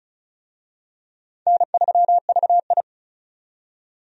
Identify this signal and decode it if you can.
Morse code, heard as a keyed tone, about 36 wpm, 700 Hz. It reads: N3VI
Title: USA Call Signs - 4 Characters - 35wpm